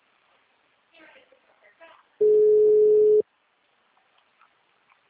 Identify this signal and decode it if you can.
Ringback tone (Europe, 425 Hz), with speech.